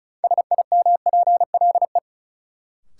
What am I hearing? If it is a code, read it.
Morse code, heard as a keyed tone, about 36 wpm, 700 Hz. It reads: SIMPLE
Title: Sentences from Top 400 Words - New Word Drill - 35wpm